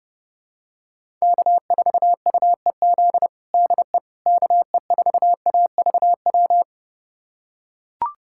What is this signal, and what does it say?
Morse code, heard as a keyed tone, about 30 wpm, 700 Hz. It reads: K4UEZ DE KE4AVW
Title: USA Callsign DE Callsign 30wpm